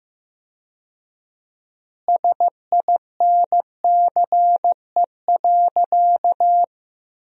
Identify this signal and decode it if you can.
Morse code, heard as a keyed tone, about 15 wpm, 700 Hz. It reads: SINCE.
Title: 4x Word-Spacing - Sentences from Top 400 Words - 15wpm